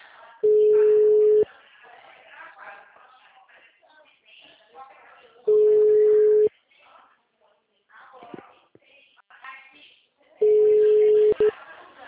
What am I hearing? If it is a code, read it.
Ringback tone (Europe, 425 Hz), with speech.